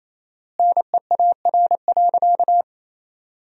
Morse code, heard as a keyed tone, about 28 wpm, 700 Hz. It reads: NEAR.
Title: 4x Word-Spacing - Sentences from Top 200 Words - 28wpm